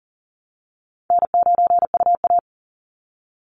Morse code, heard as a keyed tone, about 40 wpm, 700 Hz. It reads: N9UA
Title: Easy Call Signs - 40wpm